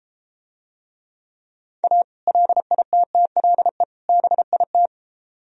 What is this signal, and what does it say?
Morse code, heard as a keyed tone, about 33 wpm, 700 Hz. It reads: A LITTLE BIT